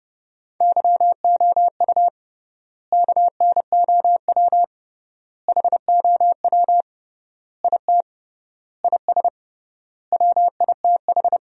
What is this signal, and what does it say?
Morse code, heard as a keyed tone, about 30 wpm, 700 Hz. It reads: YOU KNOW HOW IT IS WITH
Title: 3x Word-Spacing - Sentences from Top 100 Words - 30wpm